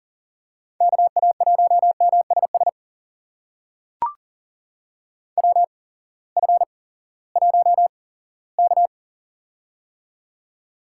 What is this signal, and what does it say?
Morse code, heard as a keyed tone, about 40 wpm, spaced as for 20 wpm, 690 Hz. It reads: KA1MSS WF1K
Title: USA Call Signs - 8x Character Spacing - 40wpm